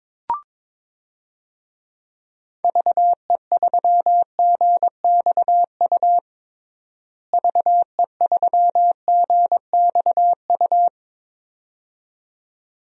Morse code, heard as a keyed tone, about 22 wpm, 700 Hz. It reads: VE3GXU VE3GXU
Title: DXCC-100 Call Signs - 6 Characters - Repeated 2x - 22wpm